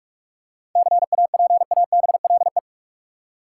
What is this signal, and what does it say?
Morse code, heard as a keyed tone, about 46 wpm, 700 Hz. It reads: CAPABLE
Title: Seven Letter Words - 45wpm